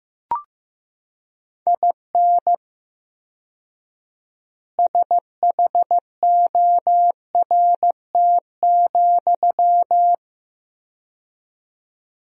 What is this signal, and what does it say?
Morse code, heard as a keyed tone, about 15 wpm, 700 Hz. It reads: IN SHORT,
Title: 4x Word-Spacing - Sentences from Top 400 Words - 15wpm